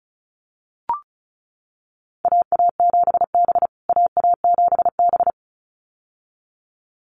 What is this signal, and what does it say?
Morse code, heard as a keyed tone, about 34 wpm, 700 Hz. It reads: AA7B AA7B
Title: USA Call Signs Repeated 2x 35wpm